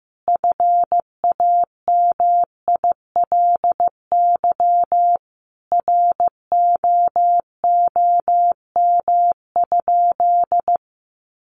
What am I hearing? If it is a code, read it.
Morse code, heard as a keyed tone, about 14 wpm, 700 Hz. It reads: FAMILY ROOM?
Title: Sentences from Top 1200 Words 15wpm